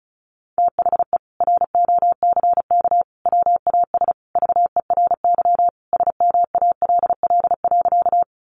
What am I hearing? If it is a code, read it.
Morse code, heard as a keyed tone, about 34 wpm, 700 Hz. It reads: THE ROCK WAS VERY SMALL.
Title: Sentences from Top 400 Words 35wpm